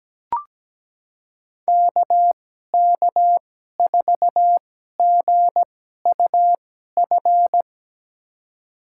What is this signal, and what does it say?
Morse code, heard as a keyed tone, about 17 wpm, spaced as for 12 wpm, 700 Hz. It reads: KK4GUF
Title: USA Call Signs - 6 Characters - 2x Character Spacing - 17wpm